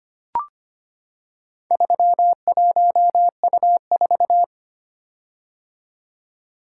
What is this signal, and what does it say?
Morse code, heard as a keyed tone, about 25 wpm, 700 Hz. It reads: 31U4
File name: practice / 25wpm / Four Letter-Number Combinations - 25wpm